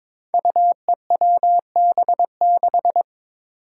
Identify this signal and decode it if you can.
Morse code, heard as a keyed tone, about 22 wpm, 700 Hz. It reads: UEWB6
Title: Five Challenging Letter-Number Combinations 22wpm